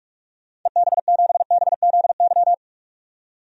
Morse code, heard as a keyed tone, about 41 wpm, 700 Hz. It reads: EB7BZY
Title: DXCC-100 Call Signs - 6 Characters - 45wpm